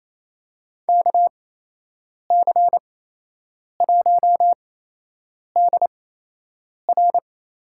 Morse code, heard as a keyed tone, about 28 wpm, spaced as for 8 wpm, 700 Hz. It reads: KC1DR
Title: USA Call Signs - 8x Character Spacing - 28wpm